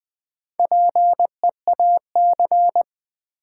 Morse code, heard as a keyed tone, about 20 wpm, 700 Hz. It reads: PEAC
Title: Build A Word - 1 Letter At A Time - 5-Letter Words - 20wpm